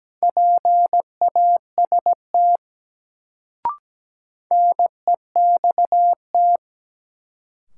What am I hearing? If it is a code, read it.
Morse code, heard as a keyed tone, about 17 wpm, 700 Hz. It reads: PAST NEXT
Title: Sets of 2 Words - New Word Drill - 17wpm